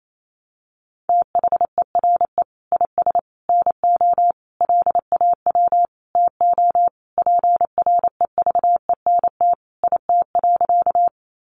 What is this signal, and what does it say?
Morse code, heard as a keyed tone, about 28 wpm, 700 Hz. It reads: THERE IS NO LAW TO PREVENT IT.
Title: Sentences from Top 1100 Words 28wpm